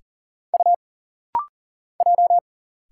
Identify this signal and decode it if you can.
Morse code, heard as a keyed tone, about 40 wpm, 700 Hz. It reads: U J
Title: Single Letters - Rapid-Fire 40wpm